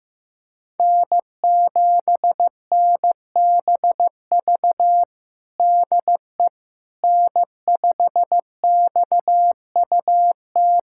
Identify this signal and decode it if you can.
Morse code, heard as a keyed tone, about 16 wpm, 700 Hz. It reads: N7NBV DE N5XUT
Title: USA Callsign DE Callsign - 5 Characters - 15wpm